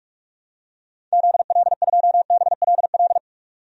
Morse code, heard as a keyed tone, about 47 wpm, 700 Hz. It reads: ZL2BLL
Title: DXCC-100 Call Signs - 6 Characters - 45wpm